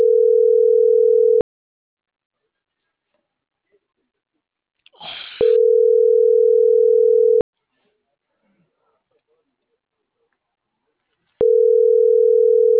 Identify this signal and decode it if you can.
Ringback tone (North America), 440 + 480 Hz.